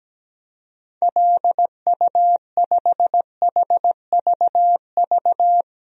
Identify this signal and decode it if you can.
Morse code, heard as a keyed tone, about 17 wpm, 700 Hz. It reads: LU5HVV
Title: DXCC-100 Call Signs - 6 Characters - 17wpm